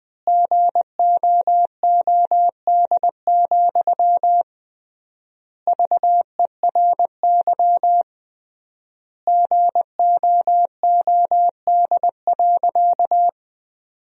Morse code, heard as a keyed tone, about 21 wpm, 700 Hz. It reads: GOOD, VERY GOOD.
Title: 3x Word-Spacing - Sentences from Top 200 Words - 20wpm